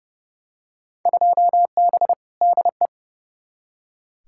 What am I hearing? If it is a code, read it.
Morse code, heard as a keyed tone, about 32 wpm, 700 Hz. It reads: 2B DE
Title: Field Day Exchanges Encore - 30wpm